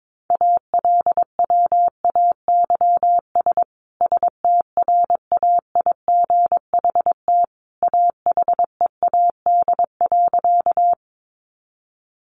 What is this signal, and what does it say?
Morse code, heard as a keyed tone, about 22 wpm, 700 Hz. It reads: ALWAYS STRAIGHT AHEAD.
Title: Sentences from Top 1100 Words 22wpm